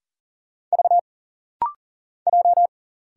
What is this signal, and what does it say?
Morse code, heard as a keyed tone, about 40 wpm, 700 Hz. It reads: V J
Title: Single Letter-Number Alt - Rapid-Fire 40wpm